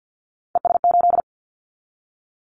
Morse code, heard as a keyed tone, about 50 wpm, 700 Hz. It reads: ES8
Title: Callsign Prefixes for Top 100 DXCC - 50wpm